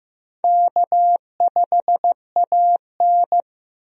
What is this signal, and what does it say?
Morse code, heard as a keyed tone, about 15 wpm, 700 Hz. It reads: K5AN
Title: USA Call Signs - 4 Characters - Repeated 2x - 15wpm